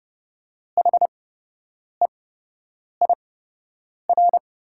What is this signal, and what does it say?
Morse code, heard as a keyed tone, about 30 wpm, spaced as for 8 wpm, 700 Hz. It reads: HEIR
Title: 8x Character Spacing - Top 100 Words - 30wpm